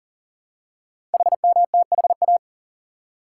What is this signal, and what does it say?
Morse code, heard as a keyed tone, about 42 wpm, 690 Hz. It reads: HMTHA
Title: Code Groups - 5 Letters - 40wpm